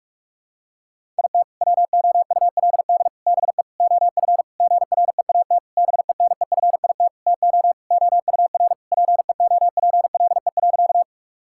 Morse code, heard as a keyed tone, about 46 wpm, 700 Hz. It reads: IT WOULD BE OF GREAT BENEFIT TO OUR PEOPLE.
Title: Sentences from Top 1000 Words - Review Part 6 - 45wpm